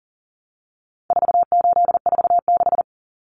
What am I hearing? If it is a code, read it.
Morse code, heard as a keyed tone, about 40 wpm, 700 Hz. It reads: 4846
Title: Common Cut Numbers - 4-Digit 40wpm